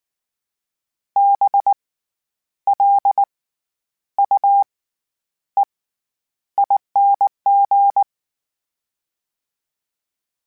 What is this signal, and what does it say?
Morse code, heard as a keyed tone, about 19 wpm, spaced as for 9 wpm, 800 Hz. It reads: BLUEING
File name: Special INGs / Special INGs-mp3s / SlowToFaster ing @18